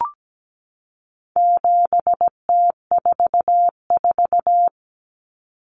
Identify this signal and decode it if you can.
Morse code, heard as a keyed tone, about 17 wpm, 700 Hz. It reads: 7T44
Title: Common Cut Numbers - 4-Digit 17wpm